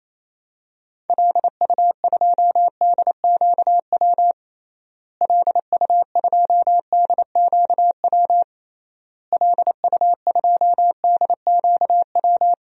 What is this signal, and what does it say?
Morse code, heard as a keyed tone, about 28 wpm, 700 Hz. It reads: LU2DQW LU2DQW LU2DQW
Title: DXCC-100 Call Signs - 6 Characters - Repeated 3x - 28wpm